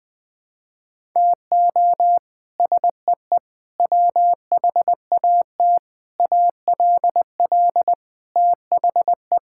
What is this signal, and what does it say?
Morse code, heard as a keyed tone, about 21 wpm, 700 Hz. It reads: TO SEE WHAT ALL THE
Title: Sets of 8 Words - 20wpm